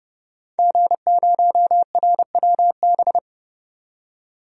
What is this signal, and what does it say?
Morse code, heard as a keyed tone, about 29 wpm, 700 Hz. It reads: G0RWB
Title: DXCC-100 Call Signs - 5 Characters - 30wpm